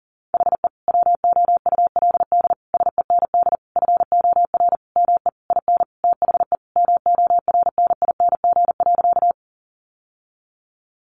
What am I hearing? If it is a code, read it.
Morse code, heard as a keyed tone, about 40 wpm, 700 Hz. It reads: HE WOULD SEND FOR ME IN THE MORNING.
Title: Sentences from Top 1500 Words 40wpm